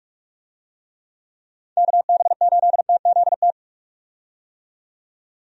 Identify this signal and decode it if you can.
Morse code, heard as a keyed tone, about 46 wpm, 700 Hz. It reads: KB8TZT
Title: USA Call Signs - 6 Characters - 45wpm